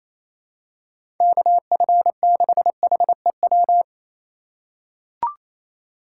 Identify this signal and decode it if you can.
Morse code, heard as a keyed tone, about 28 wpm, 700 Hz. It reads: KF6HEW
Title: USA Call Signs - 6 Characters - 4x Character Spacing - 28wpm